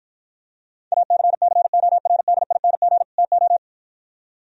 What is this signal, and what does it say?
Morse code, heard as a keyed tone, about 53 wpm, 700 Hz. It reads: ACCORDING TO